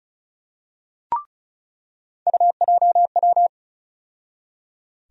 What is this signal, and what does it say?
Morse code, heard as a keyed tone, about 34 wpm, 700 Hz. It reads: UJW
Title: USA Call Sign Postfixes - 3 Letters 35wpm